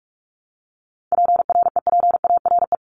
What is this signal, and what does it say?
Morse code, heard as a keyed tone, about 44 wpm, 700 Hz. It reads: PREPARE
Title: ED Suffix - Top 100 Words with Base Words 45wpm